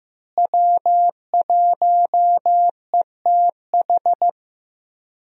Morse code, heard as a keyed tone, about 15 wpm, 700 Hz. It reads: W1ETH
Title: USA Call Signs - 5 Characters - Repeated 2x - 15wpm